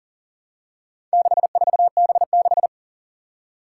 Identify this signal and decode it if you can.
Morse code, heard as a keyed tone, about 40 wpm, 700 Hz. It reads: 64B6
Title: Four Challenging Letter-Number Combinations 40wpm